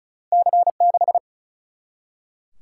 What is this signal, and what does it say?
Morse code, heard as a keyed tone, about 36 wpm, 700 Hz. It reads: C 6
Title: Single-Letter Single-Number 35wpm